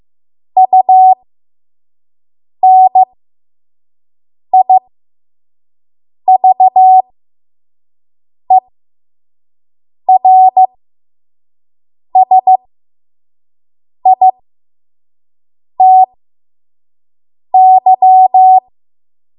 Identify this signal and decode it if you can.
Morse code, heard as a keyed tone, about 15 wpm, spaced as for 5 wpm, 750 Hz. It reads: UNIVERSITY
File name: Morse / 05WPM